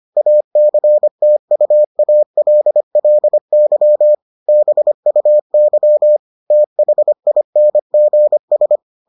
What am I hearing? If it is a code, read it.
Morse code, heard as a keyed tone, about 25 wpm, 600 Hz. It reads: ACTUALLY BUY THINGS